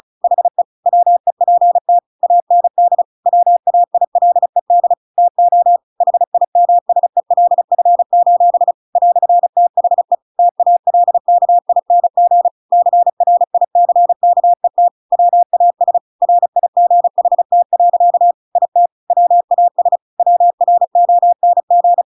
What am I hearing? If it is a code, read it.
Morse code, heard as a keyed tone, about 34 wpm, 700 Hz. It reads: HE WEPT AND WAILED TO HIMSELF: "THE TALKING CRICKET WAS RIGHT. IT WAS WRONG